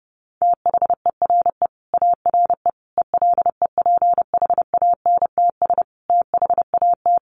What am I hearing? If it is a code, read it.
Morse code, heard as a keyed tone, about 30 wpm, 700 Hz. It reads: THERE ARE ELEPHANTS THAT
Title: Easy Phrases 30wpm